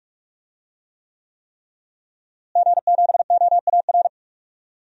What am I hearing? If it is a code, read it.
Morse code, heard as a keyed tone, about 46 wpm, 700 Hz. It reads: G7OAR
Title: DXCC-100 Call Signs - 5 Characters - Repeated 3x - 45wpm